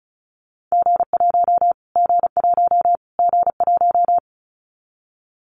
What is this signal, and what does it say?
Morse code, heard as a keyed tone, about 34 wpm, 700 Hz. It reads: G1 G1 G1
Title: Callsign Prefixes for Top 25 DXCC - Repeated 3x - 35wpm